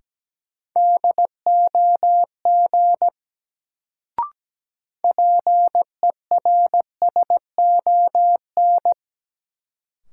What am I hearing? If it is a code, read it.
Morse code, heard as a keyed tone, about 17 wpm, 700 Hz. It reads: DOG PERSON
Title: Sentences from Top 400 Words - New Word Drill - 17wpm